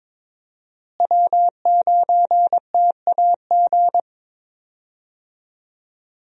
Morse code, heard as a keyed tone, about 21 wpm, 700 Hz. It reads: W9TAG
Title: USA Call Signs - 5 Characters - Repeated 3x - 22wpm